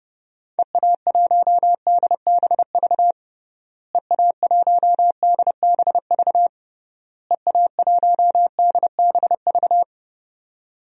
Morse code, heard as a keyed tone, about 30 wpm, 700 Hz. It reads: EA1DBV EA1DBV EA1DBV
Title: DXCC-100 Call Signs - 6 Characters - Repeated 3x - 30wpm